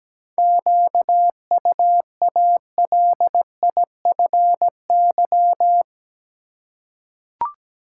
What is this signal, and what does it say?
Morse code, heard as a keyed tone, about 17 wpm, 700 Hz. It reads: QUALIFY
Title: Seven Letter Words - 17wpm